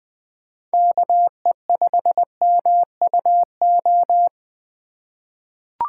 Morse code, heard as a keyed tone, about 20 wpm, 700 Hz. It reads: KE5MUO
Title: USA Call Signs - 3x Character Spacing - 20wpm